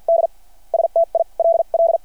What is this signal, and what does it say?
Morse code, heard as a keyed tone, about 53 wpm, spaced as for 40 wpm, 690 Hz. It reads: D STILL
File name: archived / CRSnet